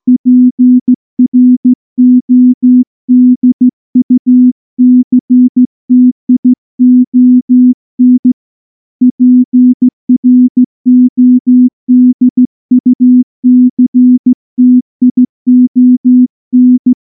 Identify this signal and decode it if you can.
Morse code, heard as a keyed tone, about 14 wpm, 250 Hz. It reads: PRODUCTION PRODUCTION
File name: morse / morse-training-15WPM